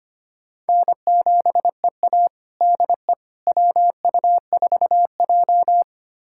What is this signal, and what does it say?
Morse code, heard as a keyed tone, about 25 wpm, 700 Hz. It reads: N7EA DE WU4J
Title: USA Callsign DE Callsign - 4 Characters - 25wpm